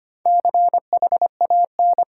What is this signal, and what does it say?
Morse code, heard as a keyed tone, about 25 wpm, 700 Hz. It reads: CHAN
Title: Sentences from Top 800 Words - New Word Drill - 25wpm